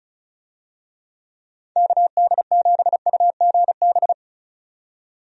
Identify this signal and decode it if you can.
Morse code, heard as a keyed tone, about 34 wpm, 700 Hz. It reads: KD7UGB
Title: USA Call Signs - 6 Characters - 35wpm